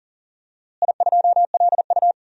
Morse code, heard as a keyed tone, about 40 wpm, 700 Hz. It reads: I2LU
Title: DXCC-100 Call Signs - 4 Characters - Repeated 2x - 40wpm